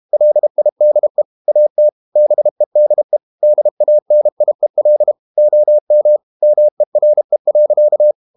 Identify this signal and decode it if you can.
Morse code, heard as a keyed tone, about 32 wpm, 600 Hz. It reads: LIDE AT BEDE DANIEL OM MERE.
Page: Ferie 32 wpm